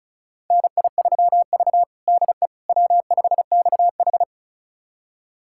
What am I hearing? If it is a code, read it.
Morse code, heard as a keyed tone, about 36 wpm, 700 Hz. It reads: NI3V DE W5XH
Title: USA Callsign DE Callsign - 4 Characters - 35wpm